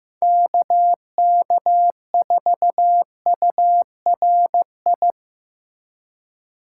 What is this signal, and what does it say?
Morse code, heard as a keyed tone, about 15 wpm, 700 Hz. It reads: KK4URI
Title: USA Call Signs - 4x Character Spacing - 15wpm